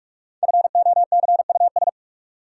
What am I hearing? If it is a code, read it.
Morse code, heard as a keyed tone, about 41 wpm, 700 Hz. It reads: FOCUS
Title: Sentences from Top 1000 Words - Review Part 4 - New Word Drill - 45wpm